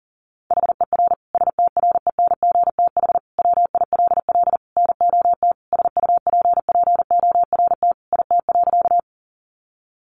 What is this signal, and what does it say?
Morse code, heard as a keyed tone, about 37 wpm, 700 Hz. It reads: HER STRENGTH WILL NOT SUPPORT IT.
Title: Sentences from Top 1300 Words 40wpm